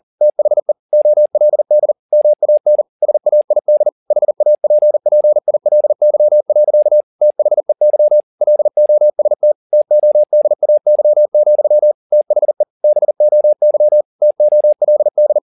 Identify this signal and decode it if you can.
Morse code, heard as a keyed tone, about 40 wpm, 600 Hz. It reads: THE OLD MAN SAID HAPPILY. THEY LOST TODAY, THE BOY TOLD